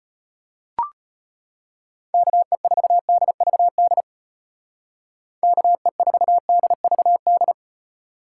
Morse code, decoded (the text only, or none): KE4DVD KE4DVD